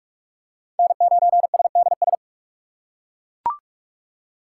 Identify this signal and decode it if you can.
Morse code, heard as a keyed tone, about 38 wpm, 700 Hz. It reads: N9SDS